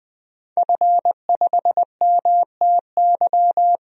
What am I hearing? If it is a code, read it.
Morse code, heard as a keyed tone, about 20 wpm, 700 Hz. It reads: F5MTY
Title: DXCC-100 Call Signs - 5 Characters - Repeated 3x - 20wpm